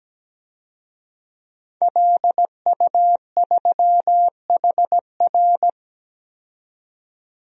Morse code, heard as a keyed tone, about 17 wpm, 700 Hz. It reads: LU3HR
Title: DXCC-100 Call Signs - 5 Characters - 17wpm